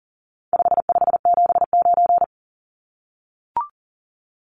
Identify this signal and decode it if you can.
Morse code, heard as a keyed tone, about 40 wpm, 700 Hz. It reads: 5579
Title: Numbers - 4-Digit 40wpm